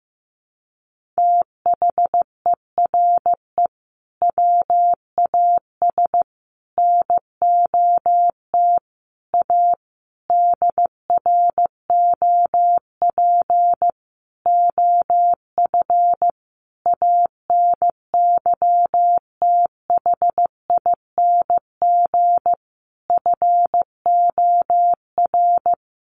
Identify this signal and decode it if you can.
Morse code, heard as a keyed tone, about 15 wpm, 700 Hz. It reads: THERE WAS NOT A DROP OF ANYTHING FOR
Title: Sentences from Top 1500 Words 15wpm